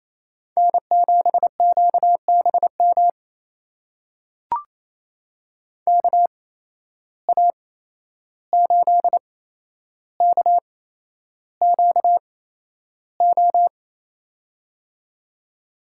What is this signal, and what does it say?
Morse code, heard as a keyed tone, about 28 wpm, spaced as for 11 wpm, 700 Hz. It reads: N7QBM KA8KQO
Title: USA Call Signs - 8x Character Spacing - 28wpm